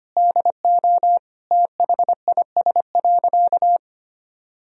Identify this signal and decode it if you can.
Morse code, heard as a keyed tone, about 25 wpm, 700 Hz. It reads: DO THIS.
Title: Sentences from Top 1000 Words - Review Part 1 - 25wpm